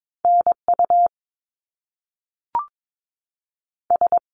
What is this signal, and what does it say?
Morse code, heard as a keyed tone, about 22 wpm, 700 Hz. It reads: NU S
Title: Canadian Provinces and Territories 22wpm